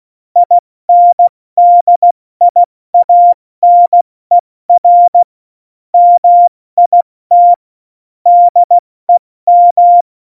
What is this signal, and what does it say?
Morse code, heard as a keyed tone, about 16 wpm, spaced as for 14 wpm, 700 Hz. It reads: INDIANER MIT DEM
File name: Morsekurs / Witze / Witz 04. September 2018, 70 F80